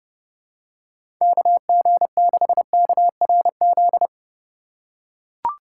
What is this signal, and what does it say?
Morse code, heard as a keyed tone, about 30 wpm, 700 Hz. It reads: KG6KRZ
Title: USA Call Signs - 6 Characters - 3x Character Spacing - 30wpm